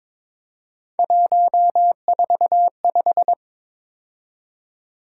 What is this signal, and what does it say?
Morse code, heard as a keyed tone, about 23 wpm, 700 Hz. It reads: 145
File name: practice / 22wpm / Three Letter-Number Combinations - 22wpm